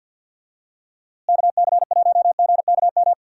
Morse code, heard as a keyed tone, about 51 wpm, 700 Hz. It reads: KC1GKM
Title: USA Call Signs - 6 Characters - Repeated 2x - 50wpm